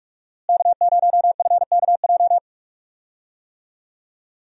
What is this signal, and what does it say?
Morse code, heard as a keyed tone, about 46 wpm, 700 Hz. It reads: K0FKJ
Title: USA Call Signs - 5 Characters - Repeated 2x - 45wpm